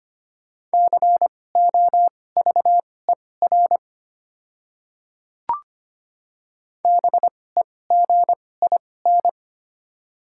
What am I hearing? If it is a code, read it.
Morse code, heard as a keyed tone, about 25 wpm, spaced as for 18 wpm, 690 Hz. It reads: COVER BEGIN
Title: 2x Character Spacing - Top 300 Words - 25wpm